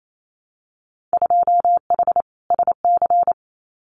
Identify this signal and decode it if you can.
Morse code, heard as a keyed tone, about 28 wpm, 700 Hz. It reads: 2H SC
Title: Winter Field Day Exchanges 28wpm